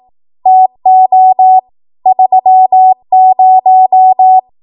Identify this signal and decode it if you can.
Morse code, heard as a keyed tone, about 18 wpm, 750 Hz. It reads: TO 30
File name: Archive / 18 WPM / 18WPM